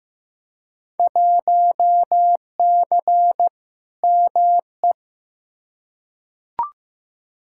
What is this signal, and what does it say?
Morse code, heard as a keyed tone, about 15 wpm, 700 Hz. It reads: 1C ME
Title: Field Day Exchanges - 15wpm